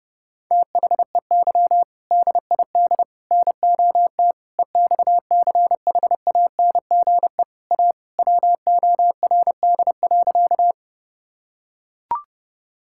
Morse code, heard as a keyed tone, about 30 wpm, 700 Hz. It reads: THEY DID NOT EXCHANGE A WORD.
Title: Sentences from Top 1000 Words - Review Part 6 - 30wpm